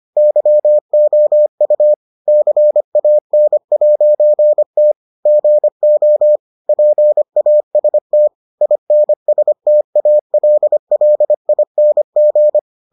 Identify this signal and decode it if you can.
Morse code, heard as a keyed tone, about 25 wpm, 600 Hz. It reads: YOU CAN'T GO PAST INSTALLING